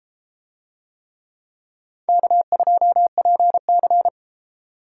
Morse code, heard as a keyed tone, about 33 wpm, 700 Hz. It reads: K2PC